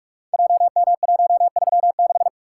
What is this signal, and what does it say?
Morse code, heard as a keyed tone, about 44 wpm, 700 Hz. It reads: JM136
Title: Code Groups - 5 Letter:Numbers - 45wpm